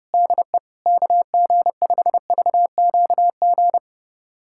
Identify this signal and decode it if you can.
Morse code, heard as a keyed tone, about 30 wpm, 700 Hz. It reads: DE KG5VQG
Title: USA Callsign DE Callsign 30wpm